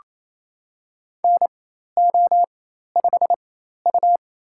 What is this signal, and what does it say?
Morse code, heard as a keyed tone, about 28 wpm, spaced as for 13 wpm, 700 Hz. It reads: NO5U
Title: USA Call Signs - 4 Characters - 4x Character Spacing - 28wpm